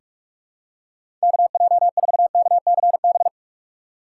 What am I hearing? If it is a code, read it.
Morse code, heard as a keyed tone, about 42 wpm, 700 Hz. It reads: KJ4KCB
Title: USA Call Signs - 6 Characters - 3x Character Spacing - 45wpm